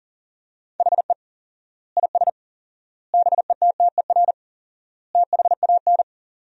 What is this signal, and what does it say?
Morse code, heard as a keyed tone, about 40 wpm, 700 Hz. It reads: HE IS BETTER THAN
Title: 4x Word-Spacing - Sentences from Top 400 Words - 40wpm